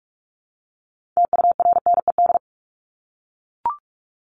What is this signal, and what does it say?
Morse code, heard as a keyed tone, about 44 wpm, 700 Hz. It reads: TURNED
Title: ED Suffix - Top 100 Words with Base Words 45wpm